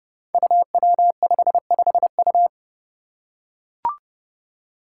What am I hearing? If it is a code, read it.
Morse code, heard as a keyed tone, about 31 wpm, 700 Hz. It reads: UW55U
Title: Five Challenging Letter-Number Combinations 30wpm